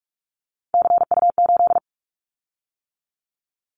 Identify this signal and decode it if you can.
Morse code, heard as a keyed tone, about 46 wpm, 700 Hz. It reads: CU8
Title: Callsign Prefixes for Top 100 DXCC - 45wpm